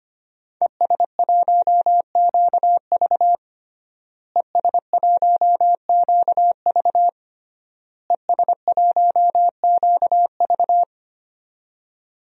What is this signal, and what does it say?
Morse code, heard as a keyed tone, about 25 wpm, 700 Hz. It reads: ES1QV ES1QV ES1QV
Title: DXCC-100 Call Signs - 5 Characters - Repeated 3x - 25wpm